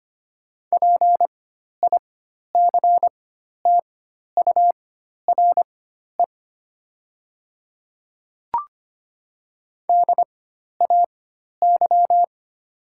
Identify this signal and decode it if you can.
Morse code, heard as a keyed tone, about 25 wpm, spaced as for 12 wpm, 700 Hz. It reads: PICTURE DAY
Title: 4x Character Spacing - Top 200 Words - 25wpm